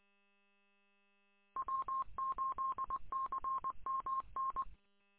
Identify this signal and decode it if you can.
Morse code, heard as a keyed tone, about 23 wpm, 1060 Hz. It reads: W8CMN